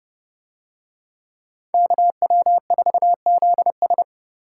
Morse code, heard as a keyed tone, about 27 wpm, 700 Hz. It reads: KW4ZS